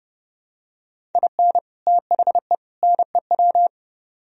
Morse code, heard as a keyed tone, about 28 wpm, 700 Hz. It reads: IN THE NEW